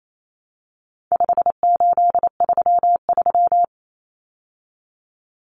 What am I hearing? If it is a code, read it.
Morse code, heard as a keyed tone, about 28 wpm, 700 Hz. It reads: 5833
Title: Numbers - 4-Digit 28wpm